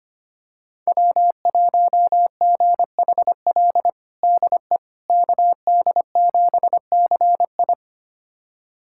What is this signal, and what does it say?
Morse code, heard as a keyed tone, about 25 wpm, 700 Hz. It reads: W1GHL DE KD7CI
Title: USA Callsign DE Callsign - 5 Characters - 25wpm